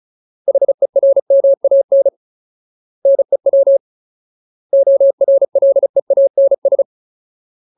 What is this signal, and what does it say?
Morse code, heard as a keyed tone, about 34 wpm, 550 Hz. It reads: HERMAN NEW ORLEANS
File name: Ultimate Word List / 35WPM Ultimate